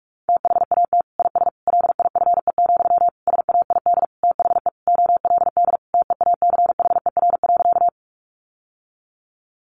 Morse code, heard as a keyed tone, about 44 wpm, 700 Hz. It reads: THAT IS LIFE, SAID THE OLD TEACHER.
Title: Sentences from Top 900 Words 45wpm